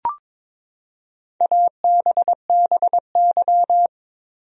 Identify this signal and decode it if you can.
Morse code, heard as a keyed tone, about 22 wpm, 700 Hz. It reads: ABBY
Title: USA Common Names - 22wpm